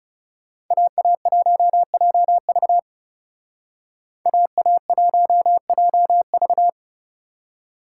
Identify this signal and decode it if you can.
Morse code, heard as a keyed tone, about 34 wpm, 700 Hz. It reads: AA1JV AA1JV
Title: Speed Racing - USA 2x2 Call Signs 30wpm